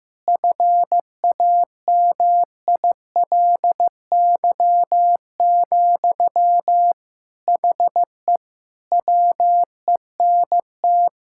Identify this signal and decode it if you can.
Morse code, heard as a keyed tone, about 15 wpm, 700 Hz. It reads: FAMILY, HE WENT
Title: Sentences from Top 1000 Words - Review Part 6 - 15wpm